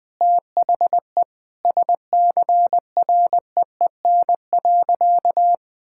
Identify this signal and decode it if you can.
Morse code, heard as a keyed tone, about 21 wpm, 700 Hz. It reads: THE SCREEN.
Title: Sentences with 6-Word Phrases - 20wpm